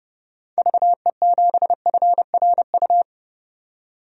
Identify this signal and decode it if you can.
Morse code, heard as a keyed tone, about 30 wpm, 700 Hz. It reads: VE7FRU
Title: DXCC-100 Call Signs - 6 Characters - 30wpm